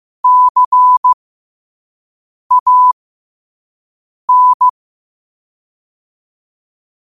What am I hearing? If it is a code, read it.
Morse code, heard as a keyed tone, about 15 wpm, spaced as for 5 wpm, 1000 Hz. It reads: CAN
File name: qotdfiles / 5wpm